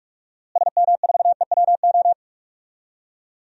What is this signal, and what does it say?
Morse code, heard as a keyed tone, about 46 wpm, 700 Hz. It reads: SM4EWO
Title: DXCC-100 Call Signs - 6 Characters - Repeated 3x - 45wpm